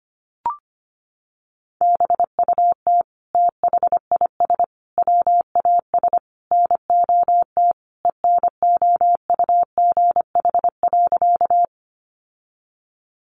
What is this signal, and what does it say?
Morse code, heard as a keyed tone, about 25 wpm, 700 Hz. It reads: BUT THIS WAS NOT ENOUGH.
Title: Sentences from Top 600 Words 25wpm